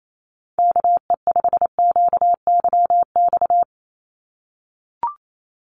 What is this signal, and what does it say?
Morse code, heard as a keyed tone, about 28 wpm, 700 Hz. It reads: KE5QYX
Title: USA Call Signs 28wpm